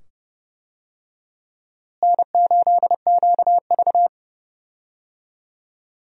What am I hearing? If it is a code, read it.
Morse code, heard as a keyed tone, about 30 wpm, 700 Hz. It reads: N8QV
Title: USA Call Signs - 4 Characters - Repeated 2x - 30wpm